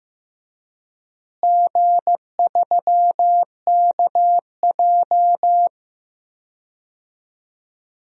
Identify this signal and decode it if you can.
Morse code, heard as a keyed tone, about 15 wpm, 700 Hz. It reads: G3KJ
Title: DXCC-100 Call Signs - 4 Characters - Repeated 2x - 15wpm